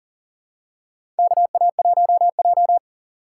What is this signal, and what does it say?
Morse code, heard as a keyed tone, about 40 wpm, 690 Hz. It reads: KA1J
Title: USA Call Signs - 4 Characters - 2x Character Spacing - 40wpm